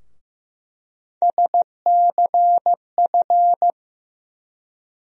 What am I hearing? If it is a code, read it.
Morse code, heard as a keyed tone, about 15 wpm, 700 Hz. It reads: SCF
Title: USA Call Sign Postfixes - 3 Letters 15wpm